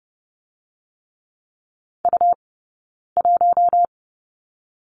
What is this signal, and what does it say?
Morse code, heard as a keyed tone, about 30 wpm, 700 Hz. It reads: U 1
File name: course / 30:10wpm / 099 Lesson 17d - Letter u - Two Characters 30wpm